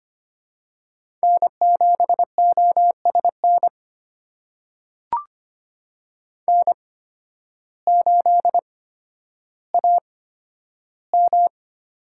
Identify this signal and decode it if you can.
Morse code, heard as a keyed tone, about 25 wpm, spaced as for 12 wpm, 700 Hz. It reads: N7OSN N8AM
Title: USA Call Signs - 5 Characters - 8x Character Spacing - 25wpm